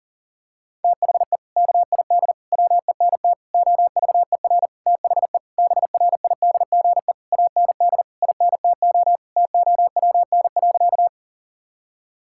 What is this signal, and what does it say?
Morse code, heard as a keyed tone, about 40 wpm, 700 Hz. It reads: THE KID WENT OVER THE BRIDGE AND INTO TOWN.
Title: Sentences from Top 1600 Words 40wpm